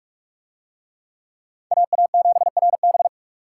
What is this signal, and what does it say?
Morse code, heard as a keyed tone, about 43 wpm, 700 Hz. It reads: AA7RB